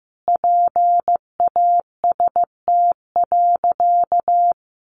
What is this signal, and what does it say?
Morse code, heard as a keyed tone, about 15 wpm, 700 Hz. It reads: PAST.
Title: Sentences from Top 1100 Words 15wpm